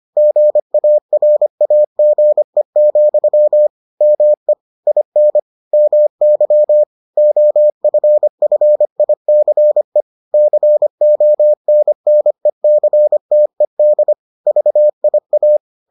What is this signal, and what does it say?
Morse code, heard as a keyed tone, about 25 wpm, 600 Hz. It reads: GARAGE, ME IN MY OFFICE CONNECTED VIA